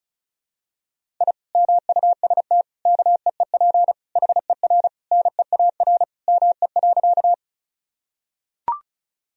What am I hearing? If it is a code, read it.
Morse code, heard as a keyed tone, about 36 wpm, 700 Hz. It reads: I MUST KEEP HER NEAR ME.
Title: Sentences from Top 1000 Words - Review Part 1 - 35wpm